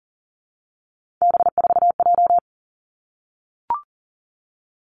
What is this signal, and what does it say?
Morse code, heard as a keyed tone, about 39 wpm, 700 Hz. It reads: B4J
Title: Three Challenging Letter-Number Combinations 40wpm